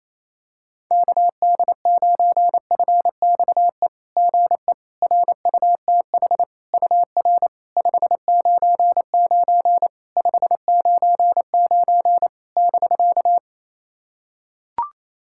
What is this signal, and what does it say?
Morse code, heard as a keyed tone, about 28 wpm, 690 Hz. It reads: KD9FXE GE RUTH UR 599 599 <BK>
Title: SOTA QSO Practice 28wpm